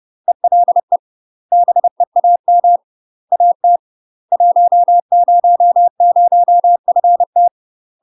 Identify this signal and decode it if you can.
Morse code, heard as a keyed tone, about 30 wpm, 700 Hz. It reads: ELE BEAM AT 100FT
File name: UK MASTER QSOs TRAINING / 30WPM / UK MASTER QSOs 0019